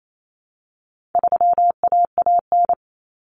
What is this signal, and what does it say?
Morse code, heard as a keyed tone, about 28 wpm, 700 Hz. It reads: 3AAN
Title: Common Cut Numbers - 4-Digit 28wpm